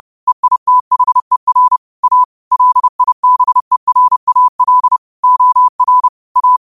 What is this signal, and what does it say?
Morse code, heard as a keyed tone, about 30 wpm, 1000 Hz. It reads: EITHER A LIBERAL OR A
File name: qotdfiles / 30wpm